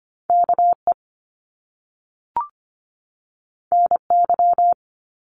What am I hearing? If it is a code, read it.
Morse code, heard as a keyed tone, about 25 wpm, 700 Hz. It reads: KE NY
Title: Top Two Letter Combinations in Words 25wpm